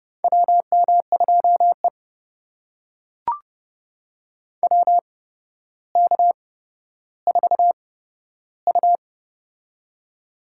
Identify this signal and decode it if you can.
Morse code, heard as a keyed tone, about 29 wpm, spaced as for 13 wpm, 700 Hz. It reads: WM2E WK4U
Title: USA Call Signs - 4 Characters - 8x Character Spacing - 30wpm